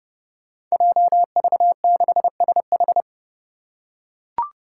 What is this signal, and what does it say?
Morse code, heard as a keyed tone, about 28 wpm, 700 Hz. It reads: JV6SH
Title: Five Challenging Letter-Number Combinations 30wpm